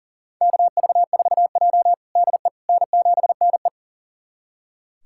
Morse code, heard as a keyed tone, about 37 wpm, 700 Hz. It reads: KV4J DE N7NE
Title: USA Callsign DE Callsign - 4 Characters - 40wpm